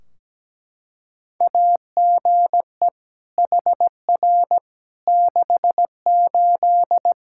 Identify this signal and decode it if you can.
Morse code, heard as a keyed tone, about 17 wpm, 700 Hz. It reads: AGE HR 68
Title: Practice QSO - Exchange 3 - More Info - 17wpm